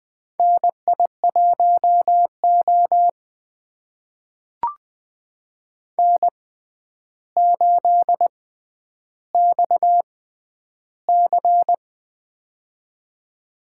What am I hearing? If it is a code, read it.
Morse code, heard as a keyed tone, about 20 wpm, spaced as for 10 wpm, 700 Hz. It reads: NI1O N8XC
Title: USA Call Signs - 4 Characters - 6x Character Spacing - 20wpm